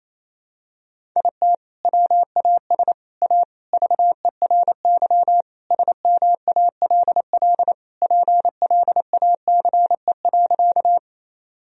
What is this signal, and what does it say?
Morse code, heard as a keyed tone, about 28 wpm, 700 Hz. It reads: IT WAS A VERY SMALL PLACE.
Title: Sentences from Top 1000 Words - Review Part 1 - 28wpm